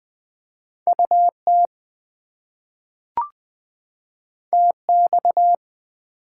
Morse code, heard as a keyed tone, about 20 wpm, 700 Hz. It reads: UT TX
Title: US States and Canadian Provinces 20wpm